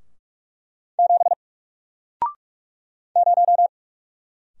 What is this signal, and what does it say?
Morse code, heard as a keyed tone, about 44 wpm, 700 Hz. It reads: 7 0
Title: Single Character (Letter:Number:Punctuation:Prosign) Alt 45wpm